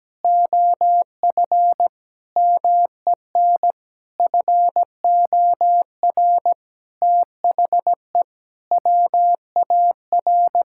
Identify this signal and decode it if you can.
Morse code, heard as a keyed tone, about 17 wpm, 700 Hz. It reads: OF MEN FOR THE WAR
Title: Sentences from Top 1000 Words - Review Part 6 - 17wpm